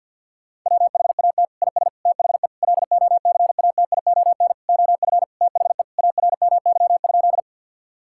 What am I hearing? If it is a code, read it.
Morse code, heard as a keyed tone, about 50 wpm, 700 Hz. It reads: WHAT IS THE LOCATION OF THE ARMY?
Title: Sentences from Top 1000 Words - Review Part 6 - 50wpm